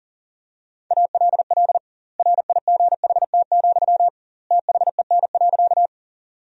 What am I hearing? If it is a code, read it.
Morse code, heard as a keyed tone, about 40 wpm, 700 Hz. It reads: ALL RIGHT, THEN.
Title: 2x Word-Spacing - Sentences from Top 200 Words - 40wpm